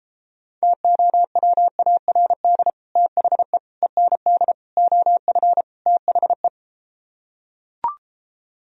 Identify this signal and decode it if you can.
Morse code, heard as a keyed tone, about 33 wpm, 700 Hz. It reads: TOWARD THE END OF THE